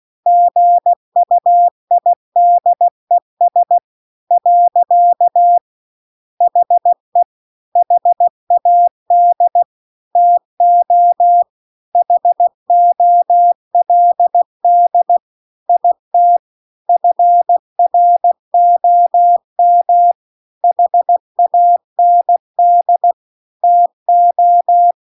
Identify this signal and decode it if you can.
Morse code, heard as a keyed tone, about 16 wpm, 700 Hz. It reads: GUIDES . HE HAD TO HOLD IT FROM HAND TO